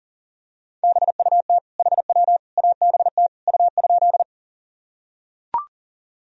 Morse code, heard as a keyed tone, about 40 wpm, 700 Hz. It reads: BUT HW ABT U?
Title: Sentences from Top 100 Words and Abbreviations 40wpm